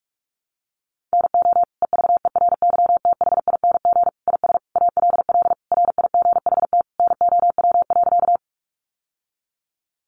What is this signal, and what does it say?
Morse code, heard as a keyed tone, about 44 wpm, 700 Hz. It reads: NO EVERYTHING IS ALL RIGHT NOW.
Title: Sentences from Top 700 Words 45wpm